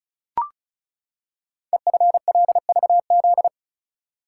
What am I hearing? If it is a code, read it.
Morse code, heard as a keyed tone, about 34 wpm, 690 Hz. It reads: EFLVZ
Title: Code Groups - 5 Letters - 35wpm